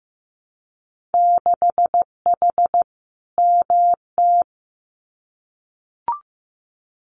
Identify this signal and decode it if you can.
Morse code, heard as a keyed tone, about 15 wpm, 700 Hz. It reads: 6H MT
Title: Winter Field Day Exchanges Encore 15wpm